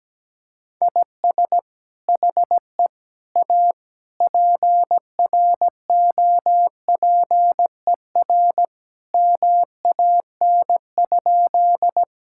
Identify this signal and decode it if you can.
Morse code, heard as a keyed tone, about 17 wpm, 700 Hz. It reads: IS HE A PROPER MAN?
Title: Sentences from Top 1000 Words - Review Part 5 - 17wpm